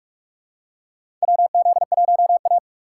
Morse code, heard as a keyed tone, about 44 wpm, 700 Hz. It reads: WZ1A
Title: USA Call Signs - 4 Characters - 3x Character Spacing - 45wpm